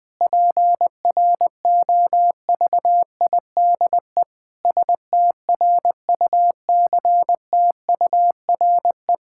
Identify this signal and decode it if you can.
Morse code, heard as a keyed tone, about 20 wpm, 700 Hz. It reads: PROVIDE STRUCTURE
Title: Sentences from Top 1000 Words - Review Part 6 - 20wpm